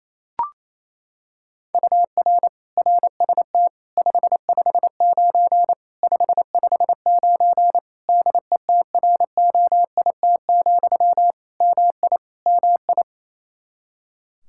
Morse code, heard as a keyed tone, about 27 wpm, 700 Hz. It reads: UR RST 559 559 DETROIT, MI MI
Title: Practice QSO - Exchange 1 - Beginning - 28wpm